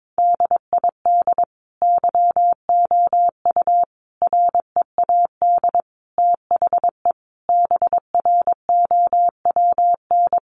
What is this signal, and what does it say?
Morse code, heard as a keyed tone, about 22 wpm, 700 Hz. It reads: DID YOU READ THE BROWN
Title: Sentences from Top 1200 Words 22wpm